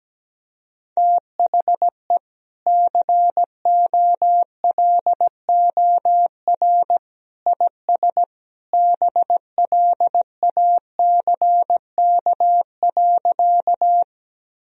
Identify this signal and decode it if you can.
Morse code, heard as a keyed tone, about 17 wpm, 700 Hz. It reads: THE COLOR IS BLACK.
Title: Sentences from Top 1000 Words - Review Part 2 - 17wpm